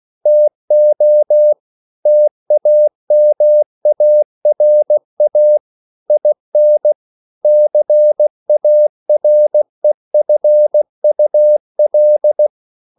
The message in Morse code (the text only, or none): TO TAMARA IN CAREFUL